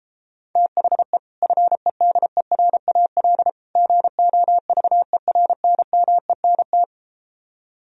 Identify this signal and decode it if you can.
Morse code, heard as a keyed tone, about 33 wpm, 700 Hz. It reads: THE FEDERAL GOVERNMENT